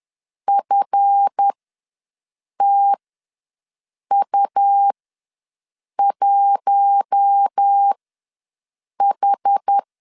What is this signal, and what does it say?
Morse code, heard as a keyed tone, about 10 wpm, spaced as for 6 wpm, 800 Hz. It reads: FTU1H
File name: LEZIONI / 7 GF9Z / 7 GF ws 10wpm 3918